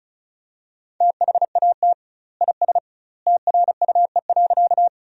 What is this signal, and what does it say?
Morse code, heard as a keyed tone, about 36 wpm, 700 Hz. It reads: THAT IS TRUE.
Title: 2x Word-Spacing - Sentences from Top 400 Words - 35wpm